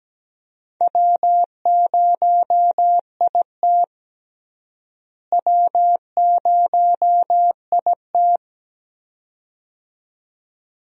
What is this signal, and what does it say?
Morse code, heard as a keyed tone, about 17 wpm, 700 Hz. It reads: W0IT W0IT
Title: USA Call Signs - 4 Characters - Repeated 2x - 17wpm